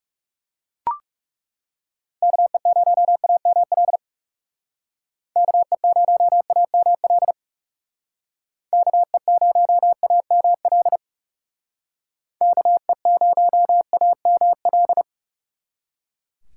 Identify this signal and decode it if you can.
Morse code, heard as a keyed tone, about 37 wpm, 700 Hz. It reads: KE0AML KE0AML KE0AML KE0AML
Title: Speed Racing - USA 2x3 Call Signs 30wpm